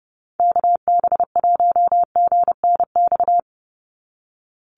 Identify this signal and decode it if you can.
Morse code, heard as a keyed tone, about 30 wpm, 700 Hz. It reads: KB1GNX
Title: USA Call Signs 30wpm